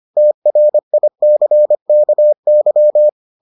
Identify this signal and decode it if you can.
Morse code, heard as a keyed tone, about 25 wpm, 600 Hz. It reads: TRICKY